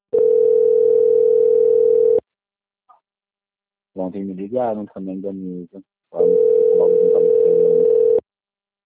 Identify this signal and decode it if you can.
Ringback tone (North America), 440 + 480 Hz, with speech.
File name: out- - - - - .28355